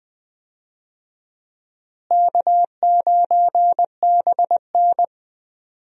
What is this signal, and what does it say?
Morse code, heard as a keyed tone, about 20 wpm, 700 Hz. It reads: K9BN
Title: USA Call Signs - 4 Characters - Repeated 2x - 20wpm